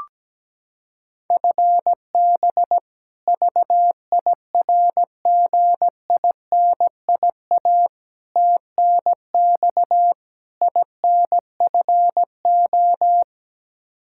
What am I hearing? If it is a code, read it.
Morse code, heard as a keyed tone, about 17 wpm, 700 Hz. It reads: FB VIRGINIA TNX INFO
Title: K1USN SST QSO Practice 17wpm